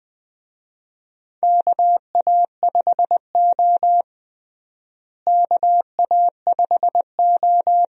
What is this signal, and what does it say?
Morse code, heard as a keyed tone, about 20 wpm, 700 Hz. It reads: KA5O KA5O
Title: USA Call Signs - 4 Characters - Repeated 2x - 20wpm